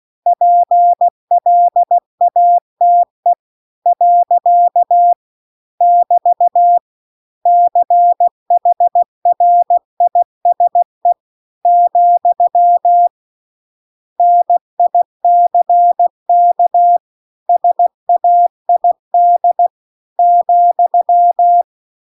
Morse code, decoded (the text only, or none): PLATE . <BT> CHRISE , NICK SAID ,